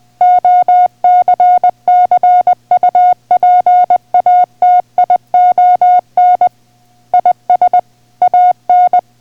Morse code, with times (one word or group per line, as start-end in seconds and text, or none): 0.2-6.5 OCCUPATION
7.1-7.8 IS
8.2-9.0 AN